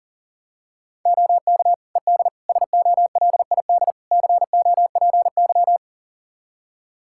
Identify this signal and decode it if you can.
Morse code, heard as a keyed tone, about 40 wpm, 700 Hz. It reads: OK ED SOLID COPY
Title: Practice QSO - Exchange 3 - More Info - 40wpm